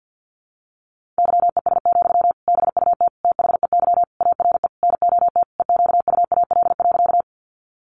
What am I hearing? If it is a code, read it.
Morse code, heard as a keyed tone, about 50 wpm, 700 Hz. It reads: YES, BUT THEY ARE NOT EQUAL.
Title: Sentences from Top 1400 Words 50wpm